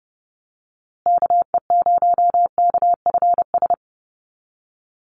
Morse code, heard as a keyed tone, about 30 wpm, 700 Hz. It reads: KE0KFS
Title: USA Call Signs Repeated 2x 30wpm